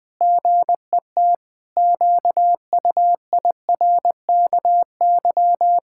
Morse code, heard as a keyed tone, about 20 wpm, 700 Hz. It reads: GET QUIRKY
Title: Pangram Challenge v2 - 20wpm